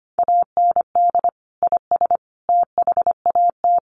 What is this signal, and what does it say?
Morse code, heard as a keyed tone, about 25 wpm, 700 Hz. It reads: AND IS THAT
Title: Sentences from Top 100 Words 25wpm